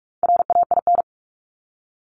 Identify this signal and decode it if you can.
Morse code, heard as a keyed tone, about 44 wpm, 700 Hz. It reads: RAIN
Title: Top 300 Words Repeated 45wpm